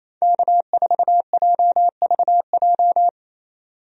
Morse code, heard as a keyed tone, about 27 wpm, 700 Hz. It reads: K4JVJ
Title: USA Call Signs - 5 Characters - 28wpm